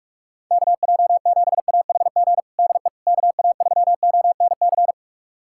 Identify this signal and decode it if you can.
Morse code, heard as a keyed tone, about 46 wpm, 700 Hz. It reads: KJ7AHG DE KA3ONC
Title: USA Callsign DE Callsign 45wpm